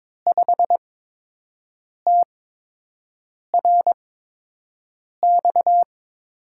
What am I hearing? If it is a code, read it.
Morse code, heard as a keyed tone, about 22 wpm, spaced as for 6 wpm, 700 Hz. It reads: 5TRX
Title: USA Call Signs - 5 Characters - 8x Character Spacing - 22wpm